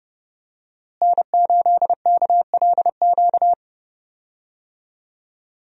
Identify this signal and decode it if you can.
Morse code, heard as a keyed tone, about 30 wpm, 700 Hz. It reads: N8KLQ
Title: USA Call Signs - 5 Characters - Repeated 3x - 30wpm